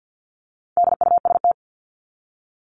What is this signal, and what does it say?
Morse code, heard as a keyed tone, about 50 wpm, 700 Hz. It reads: DUST
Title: Top 1751-2000 Words 50wpm